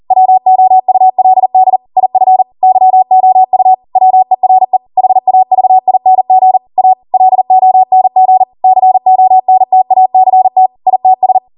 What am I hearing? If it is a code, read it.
Morse code, heard as a keyed tone, about 40 wpm, 750 Hz. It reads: WOULD IF YOU WERE HAVING A LONG CONTACT ITS